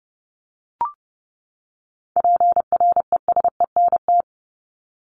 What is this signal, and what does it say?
Morse code, heard as a keyed tone, about 30 wpm, 700 Hz. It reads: PRESENT
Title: ED Suffix - Top 100 Words with Base Words 30wpm